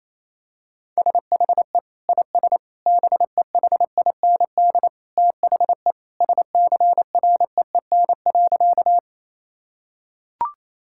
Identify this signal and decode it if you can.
Morse code, heard as a keyed tone, about 28 wpm, 700 Hz. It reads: SHE IS BEHIND THE SCREEN.
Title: Sentences from Top 1000 Words - Review Part 6 - 28wpm